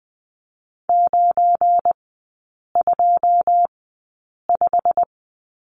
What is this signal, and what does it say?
Morse code, heard as a keyed tone, about 20 wpm, spaced as for 8 wpm, 700 Hz. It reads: 925
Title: 172 Lesson 28e - Number 9 - Three Characters 20wpm